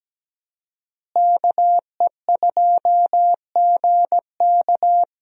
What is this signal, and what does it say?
Morse code, heard as a keyed tone, about 17 wpm, 700 Hz. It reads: KE2GK
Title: USA Call Signs - 5 Characters - Repeated 3x - 17wpm